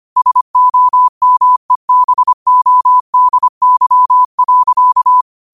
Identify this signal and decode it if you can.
Morse code, heard as a keyed tone, about 25 wpm, 1000 Hz. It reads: SOMEBODY.
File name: qotdfiles / 25wpm